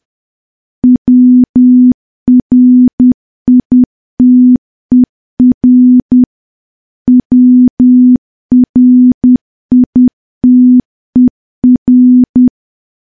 Morse code, heard as a keyed tone, about 10 wpm, 250 Hz. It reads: WRITER WRITER
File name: morse / morse-training-10WPM